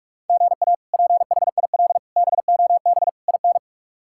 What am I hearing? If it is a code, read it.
Morse code, heard as a keyed tone, about 43 wpm, 700 Hz. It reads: GA PHIL BOB IN
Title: K1USN SST QSO Practice - 45wpm